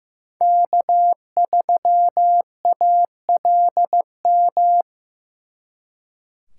Morse code, heard as a keyed tone, about 18 wpm, 700 Hz. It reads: K3ALM
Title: Speed Racing - USA 1x3 Call Signs 15wpm